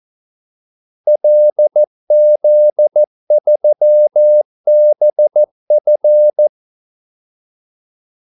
Morse code, heard as a keyed tone, about 14 wpm, 600 Hz. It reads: LZ3BF
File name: Callsigns-14wpm